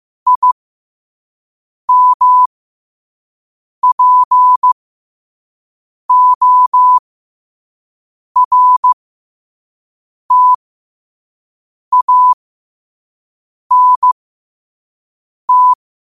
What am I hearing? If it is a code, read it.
Morse code, heard as a keyed tone, about 15 wpm, spaced as for 5 wpm, 1000 Hz. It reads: IMPORTANT
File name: qotdfiles / 5wpm